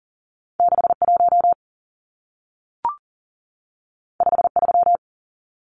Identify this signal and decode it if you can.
Morse code, heard as a keyed tone, about 40 wpm, 700 Hz. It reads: 61 53
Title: Numbers - 2-Digit 40wpm